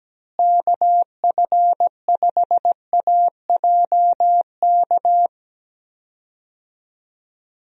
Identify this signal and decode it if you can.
Morse code, heard as a keyed tone, about 17 wpm, 700 Hz. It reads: KF5AJK
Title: USA Call Signs - 6 Characters - Repeated 2x - 17wpm